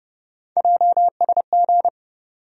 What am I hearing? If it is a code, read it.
Morse code, heard as a keyed tone, about 30 wpm, 700 Hz. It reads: JSG
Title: USA Call Sign Postfixes - 3 Letters 30wpm